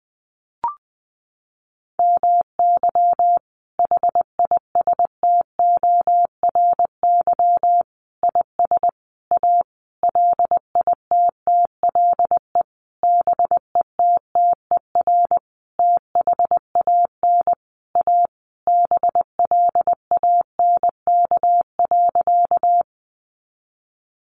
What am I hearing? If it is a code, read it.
Morse code, heard as a keyed tone, about 20 wpm, 700 Hz. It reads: MY HISTORY IS A LITTLE BETTER THAN A BLANK.
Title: Sentences from Top 1500 Words 20wpm